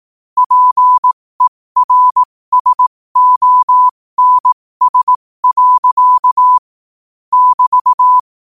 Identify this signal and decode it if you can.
Morse code, heard as a keyed tone, about 18 wpm, spaced as for 15 wpm, 1000 Hz. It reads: PERSONS. <BT>
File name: qotdfiles / 15wpm